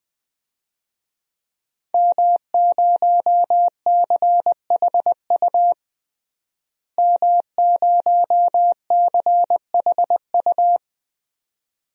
Morse code, heard as a keyed tone, about 20 wpm, 700 Hz. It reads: M0CHU M0CHU
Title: DXCC-100 Call Signs - 5 Characters - Repeated 2x - 20wpm